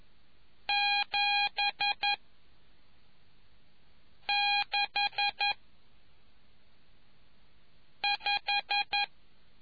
Morse code, heard as a keyed tone, about 11 wpm, 2450 Hz. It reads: 7 6 5